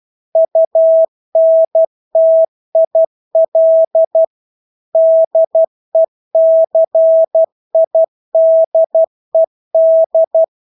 Morse code, heard as a keyed tone, about 12 wpm, 650 Hz. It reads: UNTIL DECIDED
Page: Engelske ord 12 wpm